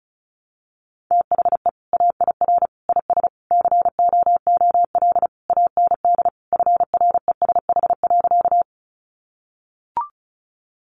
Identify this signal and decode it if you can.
Morse code, heard as a keyed tone, about 34 wpm, 700 Hz. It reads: THE AIR IS COOL AND FRESH.
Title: Sentences from Top 1200 Words 35wpm